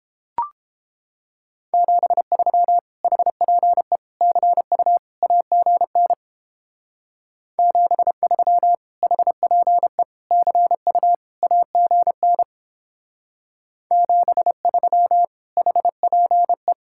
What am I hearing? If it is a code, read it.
Morse code, heard as a keyed tone, about 29 wpm, 700 Hz. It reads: 73 HPE CU AGN 73 HPE CU AGN 73 HPE